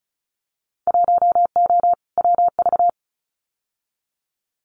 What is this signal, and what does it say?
Morse code, heard as a keyed tone, about 36 wpm, 700 Hz. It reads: 1O WV
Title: Winter Field Day Exchanges 35wpm